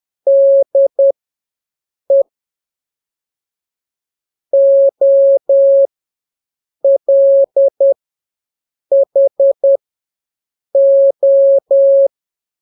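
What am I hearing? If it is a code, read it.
Morse code, heard as a keyed tone, about 10 wpm, spaced as for 6 wpm, 550 Hz. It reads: DE OLHO
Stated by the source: Telegraphy radio challenge for scouts in the Jamboree-On-The-Air.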